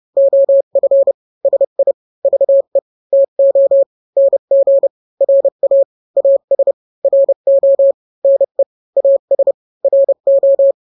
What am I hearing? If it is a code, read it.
Morse code, heard as a keyed tone, about 30 wpm, 550 Hz. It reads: OF SI VE TO NG RA AS RO NE AS RO
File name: n-grams / 2-grams 30 25 WPM